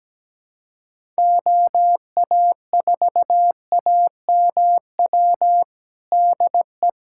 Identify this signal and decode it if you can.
Morse code, heard as a keyed tone, about 17 wpm, 700 Hz. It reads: OA4AMW DE
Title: USA Callsign DE Callsign 17wpm